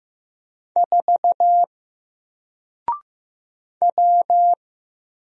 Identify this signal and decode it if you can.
Morse code, heard as a keyed tone, about 15 wpm, 700 Hz. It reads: 4 W
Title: Single Character (Letter:Number:Punctuation:Prosign) 15wpm